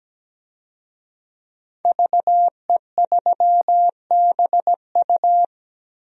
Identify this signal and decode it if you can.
Morse code, heard as a keyed tone, about 17 wpm, 700 Hz. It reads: VE3BU
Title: DXCC-100 Call Signs - 5 Characters - Repeated 3x - 17wpm